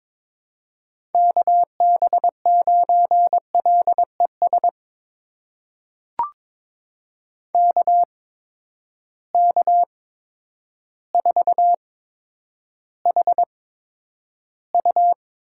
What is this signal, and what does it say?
Morse code, heard as a keyed tone, about 22 wpm, spaced as for 10 wpm, 700 Hz. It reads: KB9LES KK4HU
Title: USA Call Signs - 6 Characters - 8x Character Spacing - 22wpm